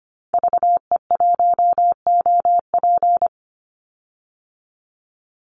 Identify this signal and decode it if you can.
Morse code, heard as a keyed tone, about 25 wpm, 700 Hz. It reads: VE1OP
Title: Call Signs Repeated 3x - 25wpm